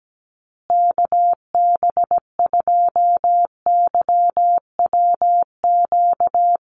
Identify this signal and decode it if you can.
Morse code, heard as a keyed tone, about 17 wpm, 700 Hz. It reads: KB2YWQ
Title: USA Call Signs Repeated 2x 17wpm